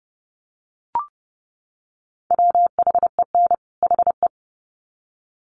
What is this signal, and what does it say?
Morse code, heard as a keyed tone, about 30 wpm, 700 Hz. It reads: WHEN HE
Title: Sets of 2 Words 30wpm